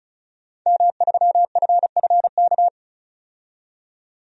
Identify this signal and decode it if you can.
Morse code, heard as a keyed tone, about 34 wpm, 700 Hz. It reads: M3FFK
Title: DXCC-100 Call Signs - 5 Characters - 35wpm